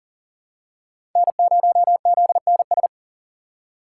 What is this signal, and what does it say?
Morse code, heard as a keyed tone, about 37 wpm, 700 Hz. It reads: N0ZNS
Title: USA Call Signs - 5 Characters - 4x Character Spacing - 40wpm